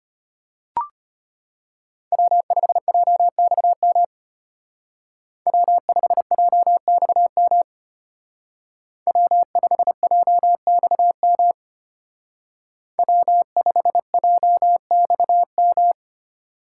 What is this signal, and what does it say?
Morse code, heard as a keyed tone, about 33 wpm, 700 Hz. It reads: W5JXM W5JXM W5JXM W5JXM